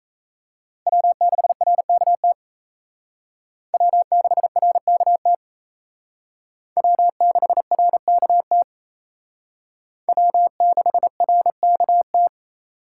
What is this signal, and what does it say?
Morse code, heard as a keyed tone, about 36 wpm, 700 Hz. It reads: W6RKT W6RKT W6RKT W6RKT